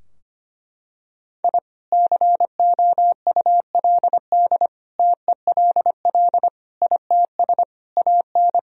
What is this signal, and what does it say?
Morse code, heard as a keyed tone, about 25 wpm, 700 Hz. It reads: I COULD TELL ITS AN
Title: Sentences from Top 1000 Words - Review Part 2 - 25wpm